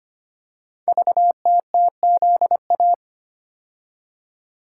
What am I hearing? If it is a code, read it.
Morse code, heard as a keyed tone, about 25 wpm, 700 Hz. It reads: VTTZA
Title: Code Groups - 5 Letters - 25wpm